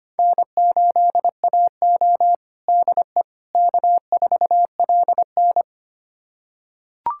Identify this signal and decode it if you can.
Morse code, heard as a keyed tone, about 25 wpm, 700 Hz. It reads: N8AO DE K4LN
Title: USA Callsign DE Callsign - 4 Characters - 25wpm